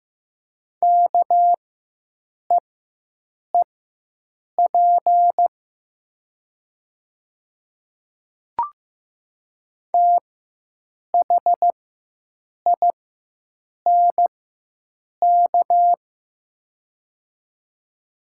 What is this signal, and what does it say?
Morse code, heard as a keyed tone, about 15 wpm, spaced as for 7 wpm, 700 Hz. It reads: KEEP THINK
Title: 4x Character Spacing - Top 300 Words - 15wpm